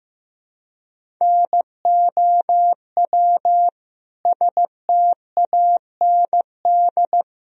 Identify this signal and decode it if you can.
Morse code, heard as a keyed tone, about 15 wpm, 700 Hz. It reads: NOW STAND
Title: Sentences from Top 1000 Words - Review Part 1 - 15wpm